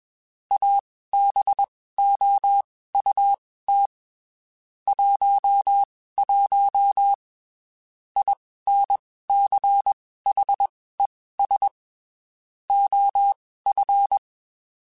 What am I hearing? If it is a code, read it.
Morse code, heard as a keyed tone, about 21 wpm, spaced as for 15 wpm, 800 Hz. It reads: ABOUT 11 INCHES OF
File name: (spaced) / QSOs Spaced-mp3s / WA7QOX 1 @20 sp